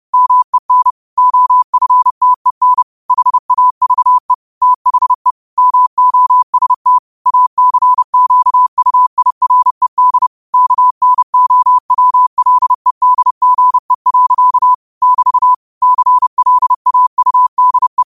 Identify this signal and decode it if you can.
Morse code, heard as a keyed tone, about 31 wpm, 1000 Hz. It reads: MEN OFTEN HAVE THE MOST ACQUIRED KNOWLEDGE. <BT> CLAUDE
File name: qotdfiles / 30wpm